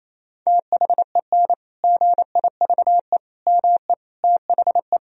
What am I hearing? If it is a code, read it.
Morse code, heard as a keyed tone, about 28 wpm, 700 Hz. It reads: THEN GIVE ME THE
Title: Sentences from Top 1000 Words - Review Part 1 - 28wpm